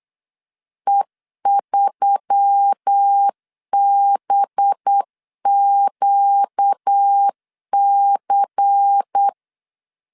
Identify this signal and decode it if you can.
Morse code, heard as a keyed tone, about 8 wpm, 800 Hz. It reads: E3BQC
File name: LEZIONI / 6 PKX0 / 6 PKX0 8wpm 3341